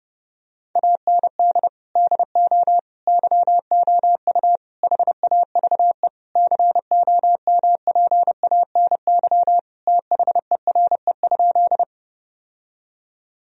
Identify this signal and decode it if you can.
Morse code, heard as a keyed tone, about 30 wpm, 700 Hz. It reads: AND DO YOU HAVE COMPANY THERE?
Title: Sentences from Top 1000 Words - Review Part 3 - 30wpm